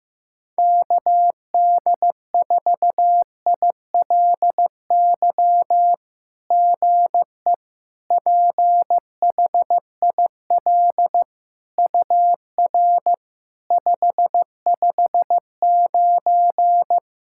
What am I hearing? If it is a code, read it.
Morse code, heard as a keyed tone, about 15 wpm, 700 Hz. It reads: KD4ILY GE PHIL UR 559
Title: SOTA QSO Practice 15wpm